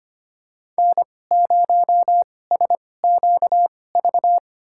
Morse code, heard as a keyed tone, about 25 wpm, spaced as for 18 wpm, 700 Hz. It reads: N0SQV
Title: USA Call Signs - 2x Character Spacing - 25wpm